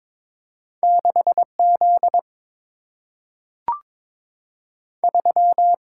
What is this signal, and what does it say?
Morse code, heard as a keyed tone, about 22 wpm, 700 Hz. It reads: 6 Z 3
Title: Single-Letter Single-Number 22wpm